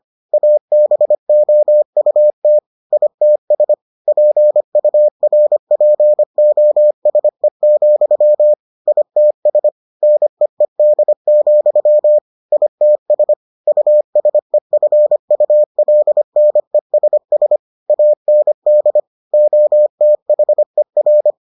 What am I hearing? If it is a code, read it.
Morse code, heard as a keyed tone, about 25 wpm, 600 Hz. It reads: ABOUT ITS PURPOSE, ITS NEED, ITS USEFULNESS AND OTHER